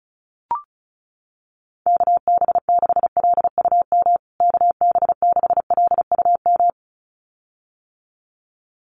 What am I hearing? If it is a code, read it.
Morse code, heard as a keyed tone, about 34 wpm, 700 Hz. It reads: KB6LUM KB6LUM
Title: USA Call Signs Repeated 2x 35wpm